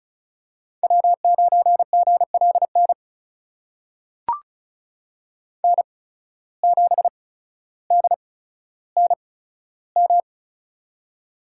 Morse code, heard as a keyed tone, about 36 wpm, spaced as for 15 wpm, 700 Hz. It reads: W9GLN N7DNM
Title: USA Call Signs - 5 Characters - 8x Character Spacing - 35wpm